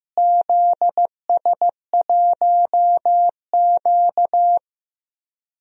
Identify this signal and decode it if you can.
Morse code, heard as a keyed tone, about 15 wpm, 700 Hz. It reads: ZS1Q
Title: DXCC-100 Call Signs - 4 Characters - Repeated 3x - 15wpm